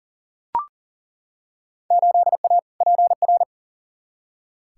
Morse code, heard as a keyed tone, about 40 wpm, 700 Hz. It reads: 8A PR
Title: Field Day Exchanges Encore - 40wpm